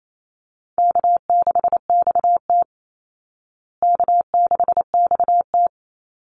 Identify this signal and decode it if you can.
Morse code, heard as a keyed tone, about 28 wpm, 700 Hz. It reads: K6XT K6XT
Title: Easy Call Signs Repeated 3x - 28wpm